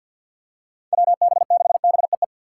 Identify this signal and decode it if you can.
Morse code, heard as a keyed tone, about 47 wpm, 700 Hz. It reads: WB6BEE
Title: K1USN SST QSO Practice - 50wpm